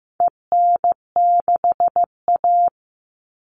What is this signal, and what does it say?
Morse code, heard as a keyed tone, about 15 wpm, 700 Hz. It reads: EN6A
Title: All Cut Numbers - 4-Digit 15wpm